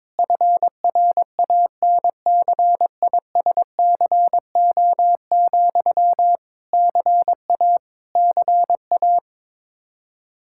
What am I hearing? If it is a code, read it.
Morse code, heard as a keyed tone, about 22 wpm, 700 Hz. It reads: FRANCISCO, CA CA
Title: Practice QSO - Exchange 1 - Beginning - 22wpm